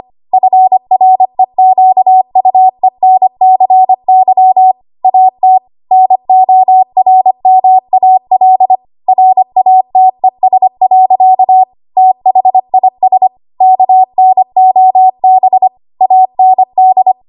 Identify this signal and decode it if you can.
Morse code, heard as a keyed tone, about 25 wpm, 750 Hz. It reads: FREQUENCY AT NORMAL RATES. THIS KNOB AND
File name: Archive / 25 WPM / 25WPM